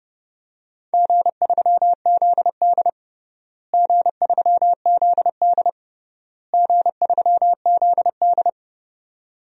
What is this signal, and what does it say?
Morse code, heard as a keyed tone, about 30 wpm, 700 Hz. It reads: G3ZD G3ZD G3ZD
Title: DXCC-100 Call Signs - 4 Characters - Repeated 3x - 30wpm